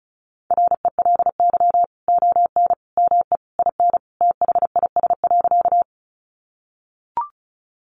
Morse code, heard as a keyed tone, about 34 wpm, 700 Hz. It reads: RELY ON ME IN THIS.
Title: Sentences from Top 1200 Words 35wpm